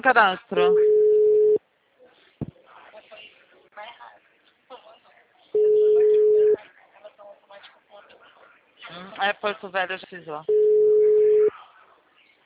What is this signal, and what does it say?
Ringback tone (Europe, 425 Hz), with speech.